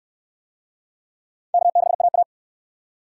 Morse code, heard as a keyed tone, about 68 wpm, 700 Hz. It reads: K6AA